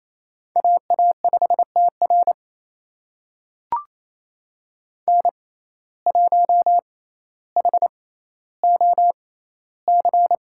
Morse code, heard as a keyed tone, about 28 wpm, spaced as for 14 wpm, 700 Hz. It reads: AA5TR N1HOC
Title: USA Call Signs - 5 Characters - 6x Character Spacing - 28wpm